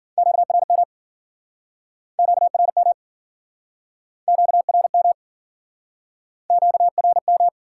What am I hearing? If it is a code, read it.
Morse code, heard as a keyed tone, about 55 wpm, 700 Hz. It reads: QRM QRM QRM QRM